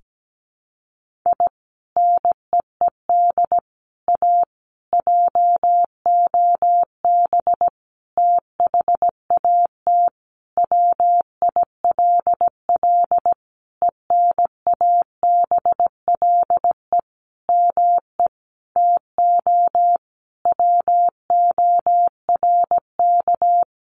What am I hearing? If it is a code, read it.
Morse code, heard as a keyed tone, about 17 wpm, 700 Hz. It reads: I NEED A JOB THAT WILL ENABLE ME TO WORK
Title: Sentences from Top 1500 Words 17wpm